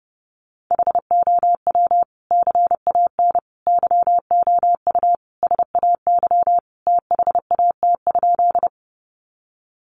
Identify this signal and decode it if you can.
Morse code, heard as a keyed tone, about 30 wpm, 700 Hz. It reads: HOW CAN YOU SAY THAT?
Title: Sentences from Top 200 Words 30wpm